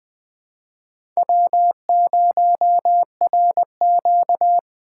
Morse code, heard as a keyed tone, about 20 wpm, 700 Hz. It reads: W0RQ
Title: USA Call Signs - 4 Characters - 6x Character Spacing - 20wpm